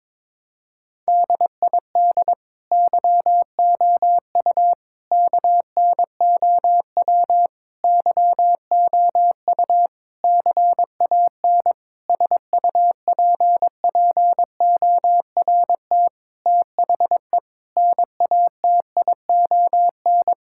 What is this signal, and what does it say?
Morse code, heard as a keyed tone, about 22 wpm, 700 Hz. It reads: DID YOU KNOW YOU CAN SUPPORT THE NATION
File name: Sets of 9 Words - 22wpm